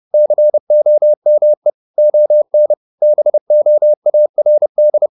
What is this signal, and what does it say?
Morse code, heard as a keyed tone, about 30 wpm, 600 Hz. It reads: COME ON BOARD
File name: the old man and the sea / the old man and the sea 1 30wpm